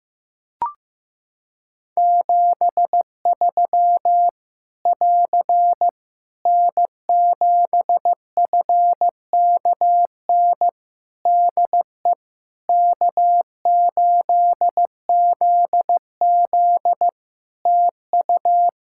Morse code, heard as a keyed tone, about 15 wpm, 700 Hz. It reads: 73 <AR> N7FKN DE K8ZZ TU
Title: Practice QSO - Exchange 4 - Ending - 15wpm